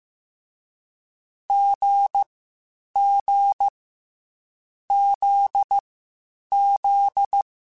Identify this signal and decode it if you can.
Morse code, heard as a keyed tone, about 15 wpm, 800 Hz. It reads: G G Z Z